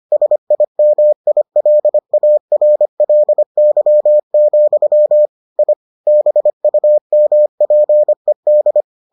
Morse code, heard as a keyed tone, about 25 wpm, 600 Hz. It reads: SIMILARLY, I BUMPED